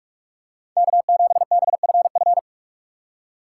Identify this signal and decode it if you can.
Morse code, heard as a keyed tone, about 46 wpm, 700 Hz. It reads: K7BFF
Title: USA Call Signs - 4x Character Spacing - 45wpm